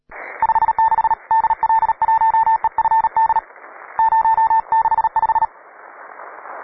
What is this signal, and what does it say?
Morse code, heard as a keyed tone, about 38 wpm, 890 Hz. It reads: 56 DL1EFD 065
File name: wae / cw pro / rt9w cw